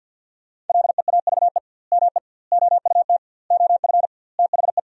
Morse code, heard as a keyed tone, about 50 wpm, 700 Hz. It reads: LEAVE ME OUT OF THE
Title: 2x Word-Spacing - Sentences from Top 400 Words - 50wpm